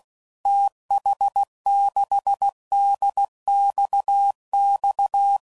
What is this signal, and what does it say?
Morse code, heard as a keyed tone, about 16 wpm, 800 Hz. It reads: TH6DXX
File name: CW Code Groups @15WPM / QSOs / WD0BGP Spaced